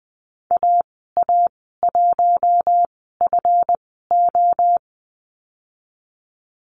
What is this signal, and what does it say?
Morse code, heard as a keyed tone, about 20 wpm, spaced as for 15 wpm, 700 Hz. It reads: AA1FO
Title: 154 Lesson 25g - Letter p - Call Signs 20wpm